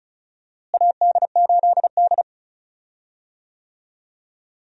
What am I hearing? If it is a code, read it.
Morse code, heard as a keyed tone, about 36 wpm, 700 Hz. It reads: AD8D
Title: USA Call Signs - 4 Characters - Repeated 2x - 35wpm